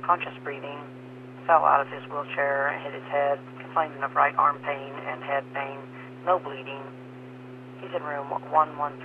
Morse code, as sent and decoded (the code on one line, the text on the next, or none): none